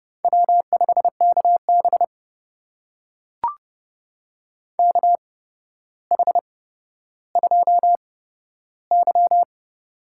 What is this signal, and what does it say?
Morse code, heard as a keyed tone, about 30 wpm, spaced as for 13 wpm, 700 Hz. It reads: W5KB KH2Y
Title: USA Call Signs - 4 Characters - 8x Character Spacing - 30wpm